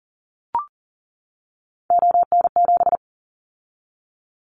Morse code, heard as a keyed tone, about 40 wpm, 700 Hz. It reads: ON7
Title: Callsign Prefixes for Top 100 DXCC - 40wpm